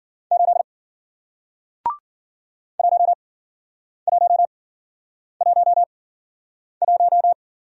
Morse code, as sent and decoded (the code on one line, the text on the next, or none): ---.. / .---- .---- .---- .----
8 1111